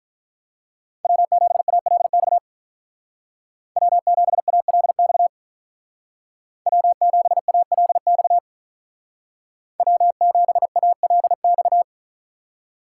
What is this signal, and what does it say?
Morse code, heard as a keyed tone, about 45 wpm, 700 Hz. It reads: W7ALX W7ALX W7ALX W7ALX